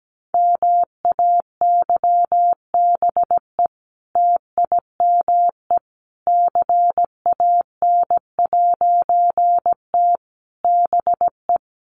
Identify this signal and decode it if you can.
Morse code, heard as a keyed tone, about 17 wpm, 700 Hz. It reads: MAYBE TIME CAN'T BE
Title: A Sound Of Thunder 17wpm